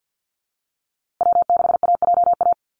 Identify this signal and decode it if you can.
Morse code, heard as a keyed tone, about 50 wpm, 700 Hz. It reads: W6AJA
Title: USA Call Signs Repeated 2x 50wpm